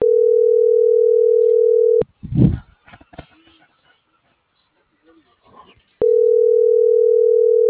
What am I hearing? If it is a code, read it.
Ringback tone (North America), 440 + 480 Hz.